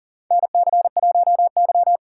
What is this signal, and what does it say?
Morse code, heard as a keyed tone, about 40 wpm, 700 Hz. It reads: NC1Y
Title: USA Call Signs - 4 Characters - 2x Character Spacing - 40wpm